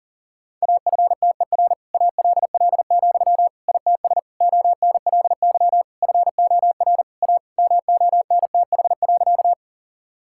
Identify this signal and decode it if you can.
Morse code, heard as a keyed tone, about 40 wpm, 700 Hz. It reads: AFTER ALL, ITS ONLY FOR A MONTH.
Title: Sentences from Top 1000 Words - Review Part 5 - 40wpm